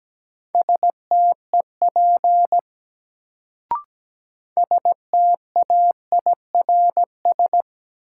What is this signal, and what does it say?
Morse code, heard as a keyed tone, about 17 wpm, 700 Hz. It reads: STEP STAIRS
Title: Sets of 5 Words Encore - New Word Drill - 17wpm